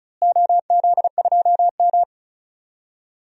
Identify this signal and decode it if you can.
Morse code, heard as a keyed tone, about 34 wpm, 700 Hz. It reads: OZ2M
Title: DXCC-100 Call Signs - 4 Characters - Repeated 2x - 35wpm